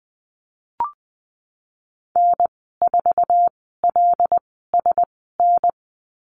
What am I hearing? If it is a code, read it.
Morse code, heard as a keyed tone, about 20 wpm, spaced as for 15 wpm, 700 Hz. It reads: N4LSN
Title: 088 Lesson 15g - Number 5 - Call Signs 20wpm